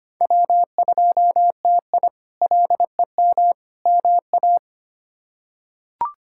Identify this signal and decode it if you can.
Morse code, heard as a keyed tone, about 25 wpm, 700 Hz. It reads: W2TI LEM MA
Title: K1USN SST QSO Practice - 25wpm